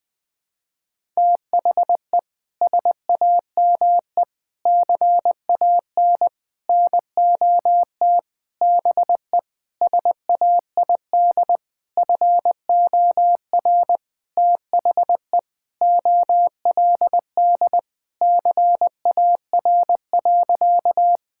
Morse code, heard as a keyed tone, about 20 wpm, 700 Hz. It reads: THE SAME CAN NOT BE SAID FOR THE OLD CAR.
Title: Sentences with 8-Word Phrases - 20wpm